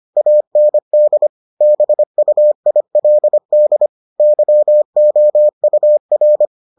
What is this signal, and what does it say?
Morse code, heard as a keyed tone, about 25 wpm, 600 Hz. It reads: AND BUILD YOUR